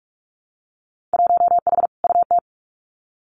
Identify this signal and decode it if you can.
Morse code, heard as a keyed tone, about 46 wpm, 700 Hz. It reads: 1H UT
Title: Winter Field Day Exchanges 45wpm